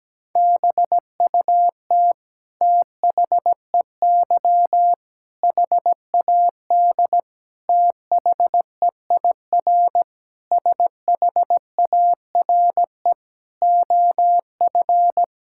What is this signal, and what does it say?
Morse code, heard as a keyed tone, about 17 wpm, 700 Hz. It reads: BUT THEY HAD THEIR SHARE OF
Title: Sentences from Top 1000 Words - Review Part 6 - 17wpm